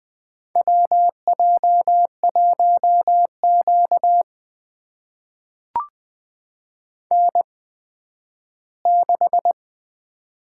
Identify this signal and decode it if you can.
Morse code, heard as a keyed tone, about 20 wpm, spaced as for 12 wpm, 700 Hz. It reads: WJ1Q N6
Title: USA Call Signs - 4 Characters - 8x Character Spacing - 20wpm